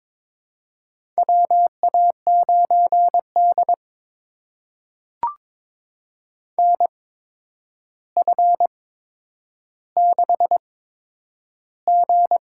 Morse code, heard as a keyed tone, about 22 wpm, spaced as for 10 wpm, 700 Hz. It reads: WA9D NF6G
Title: USA Call Signs - 4 Characters - 8x Character Spacing - 22wpm